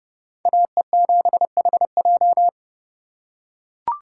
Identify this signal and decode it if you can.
Morse code, heard as a keyed tone, about 29 wpm, 700 Hz. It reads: AE7HJ